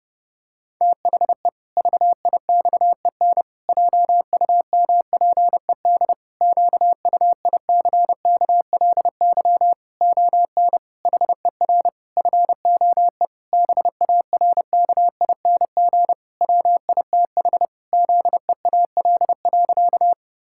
Morse code, heard as a keyed tone, about 30 wpm, 700 Hz. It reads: THE VIXEN JUMPED QUICKLY ON HER FOE BARKING WITH ZEAL.
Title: Pangram Challenge v2 - 30wpm